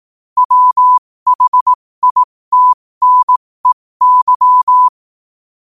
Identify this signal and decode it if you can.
Morse code, heard as a keyed tone, about 18 wpm, spaced as for 15 wpm, 1000 Hz. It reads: WHITNEY
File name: qotdfiles / 15wpm